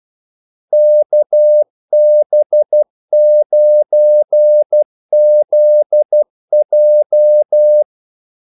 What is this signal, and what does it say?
Morse code, heard as a keyed tone, about 12 wpm, 600 Hz. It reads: KB9ZJ
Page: 2 ekstra ordmellemrum mellem kaldesignalerne